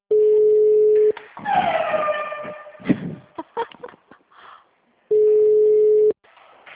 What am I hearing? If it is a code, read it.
Ringback tone (Europe, 425 Hz).
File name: out- -2015- - - .23769